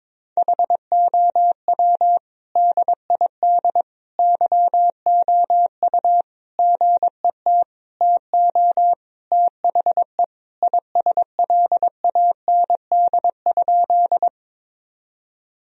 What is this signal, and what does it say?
Morse code, heard as a keyed tone, about 21 wpm, 700 Hz. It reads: HOW DID YOU GET TO THE ISLAND?
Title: Sentences from Top 1000 Words - Review Part 3 - 22wpm